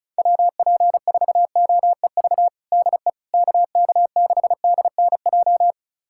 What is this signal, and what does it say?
Morse code, heard as a keyed tone, about 36 wpm, 700 Hz. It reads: WP4OEV DE KK6DNJ
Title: USA Callsign DE Callsign - 6 Characters - 35wpm